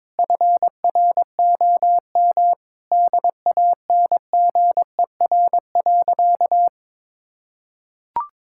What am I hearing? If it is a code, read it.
Morse code, heard as a keyed tone, about 22 wpm, 700 Hz. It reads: FROM DANGER.
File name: practice / 22wpm / Sentences with 2-Word Phrases - 22wpm